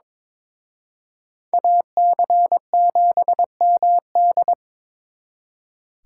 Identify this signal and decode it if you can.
Morse code, heard as a keyed tone, about 25 wpm, 700 Hz. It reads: AC7MD
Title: Speed Racing - USA 2x2 Call Signs 22wpm